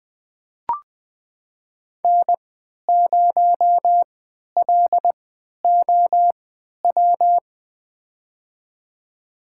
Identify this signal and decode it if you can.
Morse code, heard as a keyed tone, about 20 wpm, spaced as for 11 wpm, 700 Hz. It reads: N0LOW
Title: USA Call Signs - 5 Characters - 3x Character Spacing - 20wpm